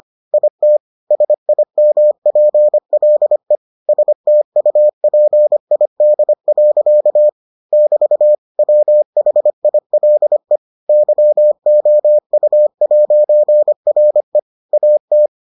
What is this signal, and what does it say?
Morse code, heard as a keyed tone, about 25 wpm, 600 Hz. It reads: IT SIMPLE STUPID. <BT> WHILE YOU'RE AT